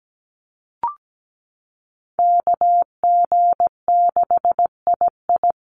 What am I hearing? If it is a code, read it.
Morse code, heard as a keyed tone, about 17 wpm, 700 Hz. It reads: KG6II
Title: Call Signs - 17wpm